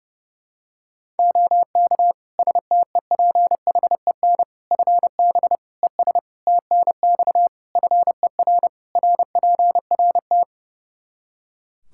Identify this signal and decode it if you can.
Morse code, heard as a keyed tone, about 30 wpm, 700 Hz. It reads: OK STEPHEN FB ES TNX FER RPRT
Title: Practice QSO - Exchange 2 - Info - 30wpm